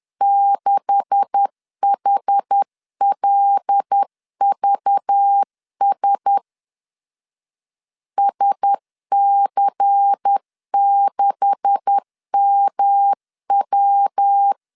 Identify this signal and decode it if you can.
Morse code, heard as a keyed tone, about 10 wpm, 800 Hz. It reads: 6HLVS SC6MW
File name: LEZIONI / 9 286 10wpm 2735